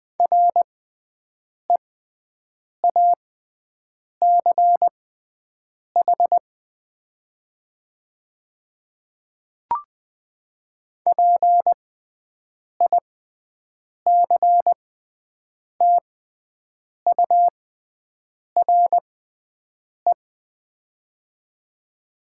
Morse code, heard as a keyed tone, about 20 wpm, spaced as for 7 wpm, 700 Hz. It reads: REACH PICTURE
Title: 6x Character Spacing - Top 500 Words - 20wpm